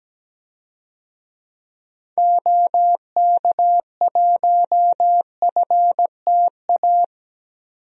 Morse code, heard as a keyed tone, about 17 wpm, 700 Hz. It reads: OK1FTA
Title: DXCC-100 Call Signs - 6 Characters - 17wpm